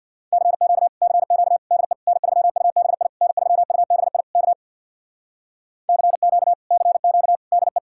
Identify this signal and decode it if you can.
Morse code, heard as a keyed tone, about 66 wpm, 700 Hz. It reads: CQ CQ DE N3UBI N3UBI K CQ CQ DE